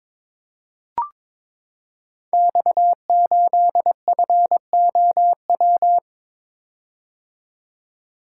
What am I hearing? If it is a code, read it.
Morse code, heard as a keyed tone, about 21 wpm, 700 Hz. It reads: X8FOW
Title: Code Groups - 5 Letter:Numbers - 22wpm